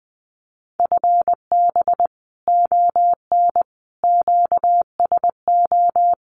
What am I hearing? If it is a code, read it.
Morse code, heard as a keyed tone, about 20 wpm, 700 Hz. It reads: FB ON QSO
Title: Common QSO Phrases 20wpm